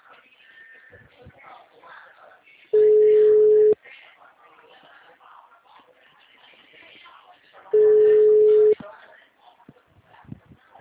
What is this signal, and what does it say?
Ringback tone (Europe, 425 Hz), with speech.